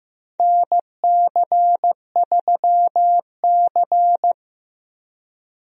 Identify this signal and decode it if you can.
Morse code, heard as a keyed tone, about 15 wpm, 700 Hz. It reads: NC3C
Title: USA Call Signs - 4 Characters - Repeated 3x - 15wpm